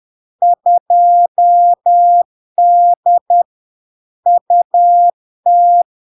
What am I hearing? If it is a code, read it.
Morse code, heard as a keyed tone, about 10 wpm, 700 Hz. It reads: 2D UT
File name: LARK-FD-2022-Log-in-CW-10wpm-700-Hz